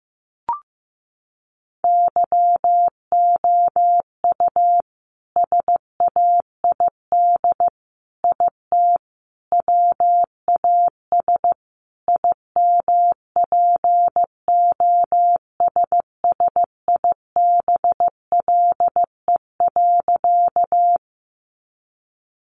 Morse code, heard as a keyed tone, about 15 wpm, 700 Hz. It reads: YOU SAID IT WAS IMPOSSIBLE.
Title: Sentences from Top 1100 Words 15wpm